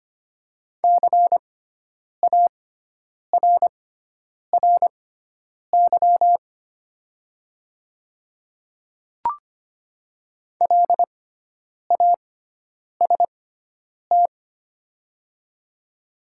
Morse code, heard as a keyed tone, about 25 wpm, spaced as for 9 wpm, 700 Hz. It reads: CARRY LAST
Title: 6x Character Spacing - Top 300 Words - 25wpm